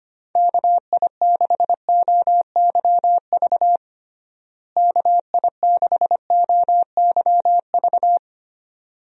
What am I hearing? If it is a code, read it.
Morse code, heard as a keyed tone, about 25 wpm, 700 Hz. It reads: KI6OYV KI6OYV
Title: USA Call Signs - 6 Characters - Repeated 2x - 25wpm